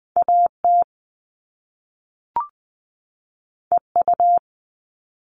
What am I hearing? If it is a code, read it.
Morse code, heard as a keyed tone, about 20 wpm, 700 Hz. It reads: AT EU
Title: All Cut Numbers - 2-Digit 20wpm